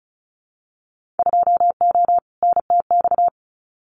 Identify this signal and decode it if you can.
Morse code, heard as a keyed tone, about 36 wpm, 700 Hz. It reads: 2O NTX
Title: Winter Field Day Exchanges 35wpm